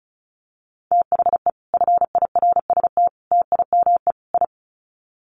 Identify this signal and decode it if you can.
Morse code, heard as a keyed tone, about 34 wpm, 700 Hz. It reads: THE FIRST TIME I
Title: Sets of 4 Words 35wpm